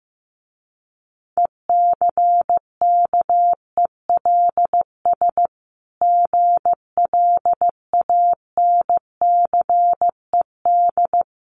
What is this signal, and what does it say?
Morse code, heard as a keyed tone, about 14 wpm, 700 Hz. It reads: ECKELS GLANCED
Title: A Sound Of Thunder 15wpm